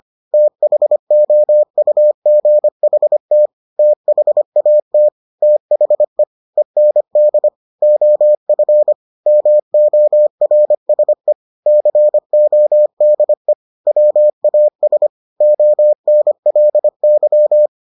Morse code, heard as a keyed tone, about 25 wpm, 600 Hz. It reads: THOUGHT THAT THE END OF MORSE CODE WAS ONLY